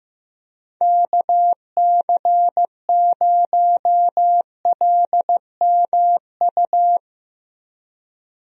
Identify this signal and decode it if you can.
Morse code, heard as a keyed tone, about 15 wpm, 700 Hz. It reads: KC0LMU
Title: USA Call Signs - 6 Characters - Repeated 3x - 15wpm